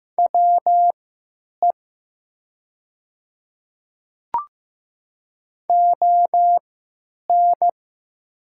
Morse code, heard as a keyed tone, about 15 wpm, spaced as for 9 wpm, 700 Hz. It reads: WE ON
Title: 3x Character Spacing - Top 100 Words - 15wpm